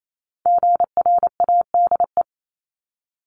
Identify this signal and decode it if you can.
Morse code, heard as a keyed tone, about 28 wpm, 700 Hz. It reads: GRADE
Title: Multi-letter Phonemes 28wpm